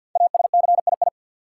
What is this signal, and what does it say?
Morse code, heard as a keyed tone, about 53 wpm, 700 Hz. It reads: ASCII
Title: Sets of 5 Words - New Word Drill - 50wpm